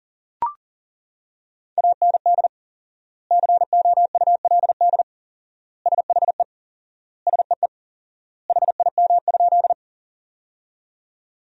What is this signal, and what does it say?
Morse code, heard as a keyed tone, about 40 wpm, 700 Hz. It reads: AND COULD SHE SEE HIM?
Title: 4x Word-Spacing - Sentences from Top 100 Words - 40wpm